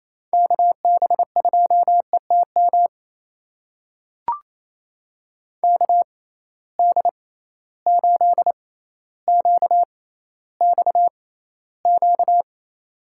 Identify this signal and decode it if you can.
Morse code, heard as a keyed tone, about 28 wpm, spaced as for 14 wpm, 700 Hz. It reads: KB2ETM KD8QXQ
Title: USA Call Signs - 6 Characters - 6x Character Spacing - 28wpm